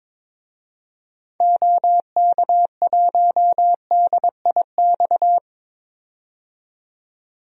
Morse code, heard as a keyed tone, about 22 wpm, 700 Hz. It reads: OK1DIX
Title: DXCC-100 Call Signs - 6 Characters - 22wpm